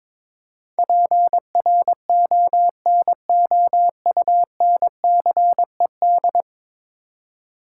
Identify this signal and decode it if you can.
Morse code, heard as a keyed tone, about 22 wpm, 700 Hz. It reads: PRONOUNCED
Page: morse-code-ninja-practice-22wpm